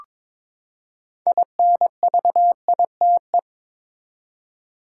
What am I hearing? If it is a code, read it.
Morse code, heard as a keyed tone, about 21 wpm, 700 Hz. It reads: INVITE
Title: Six Letter Words - 22wpm